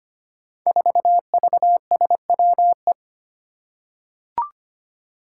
Morse code, heard as a keyed tone, about 25 wpm, 700 Hz. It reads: 4VSWE
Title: Five Challenging Letter-Number Combinations 25wpm